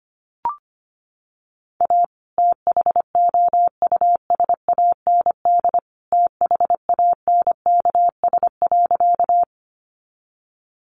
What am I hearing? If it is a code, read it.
Morse code, heard as a keyed tone, about 25 wpm, 700 Hz. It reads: A THOUSAND THANKS.
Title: Sentences from Top 600 Words 25wpm